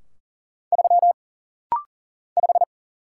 Morse code, heard as a keyed tone, about 40 wpm, 700 Hz. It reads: 3 5
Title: Single Letter-Number - Rapid-Fire 40wpm